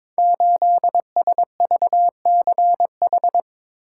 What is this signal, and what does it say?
Morse code, heard as a keyed tone, about 22 wpm, 700 Hz. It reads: 8SVCH
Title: Code Groups - 5 Letter:Numbers - 22wpm